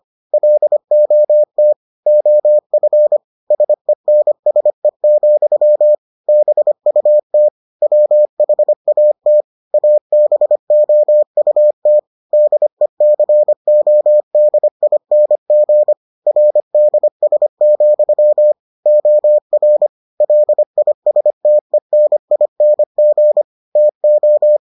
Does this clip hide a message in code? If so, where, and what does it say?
Yes: Morse code from 0.3 to 24.6 s, heard as a keyed tone, about 25 wpm, 600 Hz. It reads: LOT OF SENSE, BUT WHAT ABOUT DECODING RDS, OR LISTENING TO